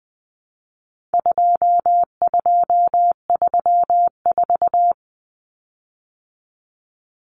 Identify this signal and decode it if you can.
Morse code, heard as a keyed tone, about 20 wpm, 700 Hz. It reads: 2234
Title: Common Cut Numbers - 4-Digit 20wpm